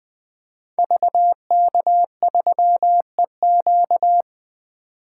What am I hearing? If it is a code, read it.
Morse code, heard as a keyed tone, about 20 wpm, 700 Hz. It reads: VK3EQ
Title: 2023 SOTA Activator Call Signs - 20wpm